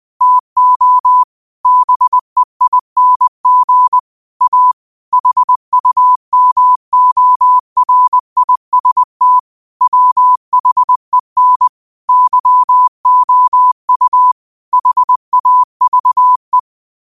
Morse code, heard as a keyed tone, about 20 wpm, 1000 Hz. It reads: TO BEING A HUMORIST WHEN YOU HAVE